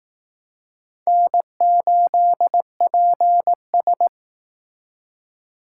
Morse code, heard as a keyed tone, about 19 wpm, 700 Hz. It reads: N8PS
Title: Speed Racing - USA 1x2 Call Signs 15wpm